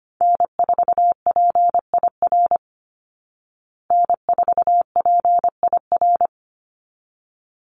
Morse code, heard as a keyed tone, about 25 wpm, 700 Hz. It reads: N4PIR N4PIR
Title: Call Signs Repeated 3x - 25wpm